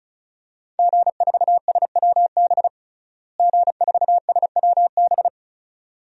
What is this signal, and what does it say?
Morse code, heard as a keyed tone, about 36 wpm, 700 Hz. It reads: G4SWB G4SWB
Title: DXCC-100 Call Signs - 5 Characters - Repeated 2x - 35wpm